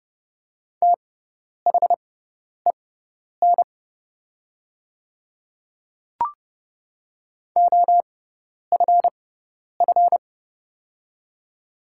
Morse code, heard as a keyed tone, about 29 wpm, spaced as for 10 wpm, 690 Hz. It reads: THEN OFF
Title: 6x Character Spacing - Top 200 Words - 30wpm